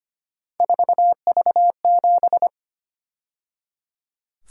Morse code, heard as a keyed tone, about 29 wpm, 700 Hz. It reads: 4V7
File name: Speed-Racing - Three Letter-Number Combinations - 25wpm